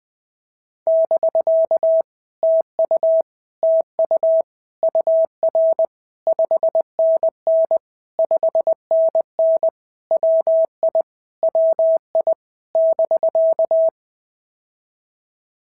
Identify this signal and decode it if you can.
Morse code, heard as a keyed tone, about 20 wpm, 650 Hz. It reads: <BK> TU TU UR 5NN 5NN WI WI <BK>
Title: POTA QSO Practice - 20wpm